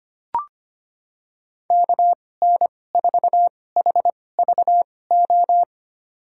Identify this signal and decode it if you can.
Morse code, heard as a keyed tone, about 25 wpm, spaced as for 18 wpm, 700 Hz. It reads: KN4HVO
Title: USA Call Signs - 6 Characters - 2x Character Spacing - 25wpm